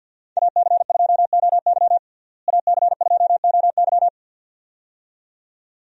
Morse code, heard as a keyed tone, about 50 wpm, 700 Hz. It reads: AC2OY AC2OY
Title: USA Call Signs - 5 Characters - Repeated 2x - 50wpm